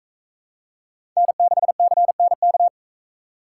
Morse code, heard as a keyed tone, about 41 wpm, 700 Hz. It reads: N6CNK